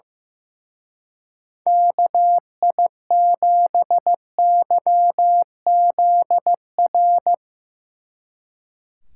Morse code, heard as a keyed tone, about 16 wpm, 700 Hz. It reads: KI7YZR
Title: Speed Racing - USA 2x3 Call Signs 15wpm